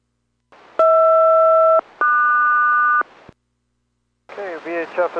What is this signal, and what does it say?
Aircraft SELCAL, heard as two pairs of tones, with speech. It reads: HR-QS